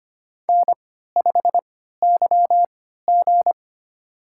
Morse code, heard as a keyed tone, about 25 wpm, spaced as for 14 wpm, 700 Hz. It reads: N5YG
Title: USA Call Signs - 4 Characters - 3x Character Spacing - 25wpm